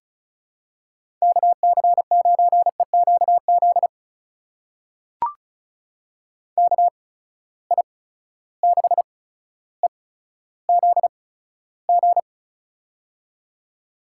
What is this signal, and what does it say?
Morse code, heard as a keyed tone, about 36 wpm, spaced as for 15 wpm, 700 Hz. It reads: KC9EQZ KI6EZG
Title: USA Call Signs - 6 Characters - 8x Character Spacing - 35wpm